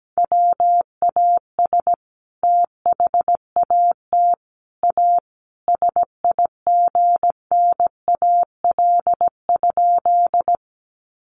Morse code, heard as a keyed tone, about 17 wpm, 700 Hz. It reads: WAS THAT A SIGNAL?
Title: Sentences from Top 1400 Words 17wpm